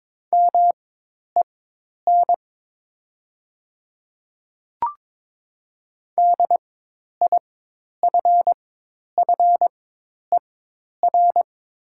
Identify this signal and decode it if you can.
Morse code, heard as a keyed tone, about 22 wpm, spaced as for 10 wpm, 700 Hz. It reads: MEN DIFFER
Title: 4x Character Spacing - Top 200 Words - 22wpm